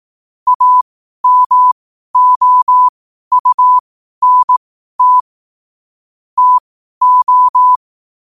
Morse code, heard as a keyed tone, about 18 wpm, spaced as for 12 wpm, 1000 Hz. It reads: AMOUNT TO
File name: qotdfiles / 12wpm